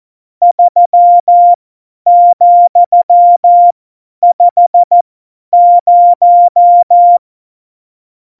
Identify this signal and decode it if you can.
Morse code, heard as a keyed tone, about 14 wpm, spaced as for 10 wpm, 700 Hz. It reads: 3,50
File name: Morsekurs / Witze / Witz 17. September 2019, 50 F70